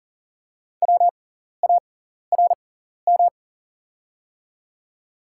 Morse code, heard as a keyed tone, about 39 wpm, spaced as for 14 wpm, 700 Hz. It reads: WARM
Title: 6x Character Spacing - Top 500 Words - 40wpm